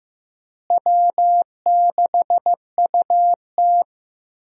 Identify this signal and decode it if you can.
Morse code, heard as a keyed tone, about 15 wpm, 700 Hz. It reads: W6UT
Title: USA Call Signs - 4 Characters - Repeated 2x - 15wpm